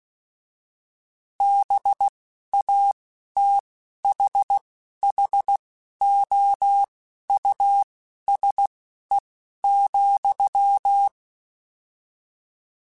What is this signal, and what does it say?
Morse code, heard as a keyed tone, about 16 wpm, spaced as for 11 wpm, 800 Hz. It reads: BATHHOUSE,
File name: General / General-mp3s / Houses @10